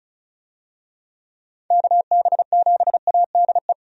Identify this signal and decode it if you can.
Morse code, heard as a keyed tone, about 34 wpm, 700 Hz. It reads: KB7ADE
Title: USA Call Signs - 6 Characters - 35wpm